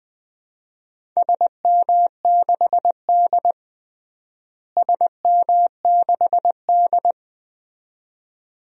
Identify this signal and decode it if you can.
Morse code, heard as a keyed tone, about 20 wpm, 700 Hz. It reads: SM6D SM6D
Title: DXCC-100 Call Signs - 4 Characters - Repeated 2x - 20wpm